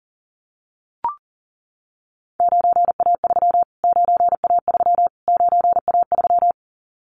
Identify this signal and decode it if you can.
Morse code, heard as a keyed tone, about 40 wpm, 700 Hz. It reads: 9A3 9A3 9A3
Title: Callsign Prefixes for Top 25 DXCC - Repeated 3x - 40wpm